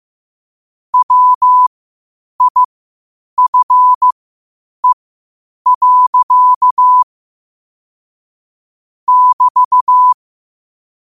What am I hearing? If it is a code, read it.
Morse code, heard as a keyed tone, about 15 wpm, spaced as for 8 wpm, 1000 Hz. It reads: WIFE. <BT>
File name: qotdfiles / 8wpm